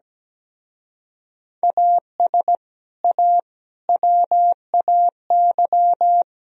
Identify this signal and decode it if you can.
Morse code, heard as a keyed tone, about 19 wpm, 700 Hz. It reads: AS A WAY
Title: Speed-Racing - Sets of 4 Words 17wpm